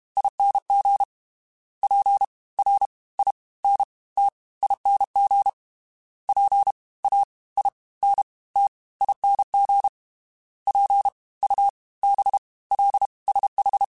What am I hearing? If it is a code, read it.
Morse code, heard as a keyed tone, about 32 wpm, 800 Hz. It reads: ING P R I N T ING P A I N T ING P U B L SH
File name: Special INGs / Special INGs-mp3s / Fast ing @30